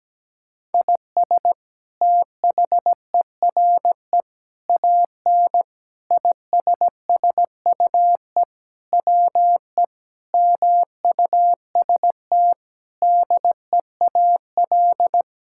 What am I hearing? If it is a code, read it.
Morse code, heard as a keyed tone, about 17 wpm, 700 Hz. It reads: IS THERE AN ISSUE WE MUST DEAL
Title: Sentences from Top 1000 Words - Review Part 6 - 17wpm